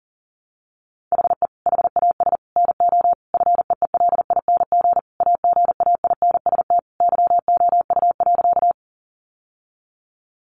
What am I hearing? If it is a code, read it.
Morse code, heard as a keyed tone, about 37 wpm, 700 Hz. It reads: HE HAS NO FEELING AGAINST YOU.
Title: Sentences from Top 1200 Words 40wpm